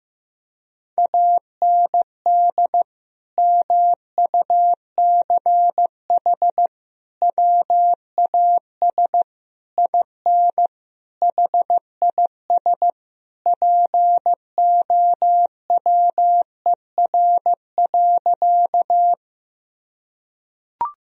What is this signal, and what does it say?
Morse code, heard as a keyed tone, about 15 wpm, 700 Hz. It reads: AND MUCH WAS IN HIS POWER.
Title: Sentences from Top 1000 Words - Review Part 2 - 15wpm